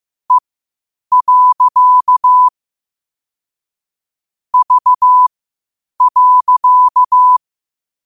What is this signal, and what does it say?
Morse code, heard as a keyed tone, about 15 wpm, spaced as for 8 wpm, 1000 Hz. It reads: E. V.
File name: qotdfiles / 8wpm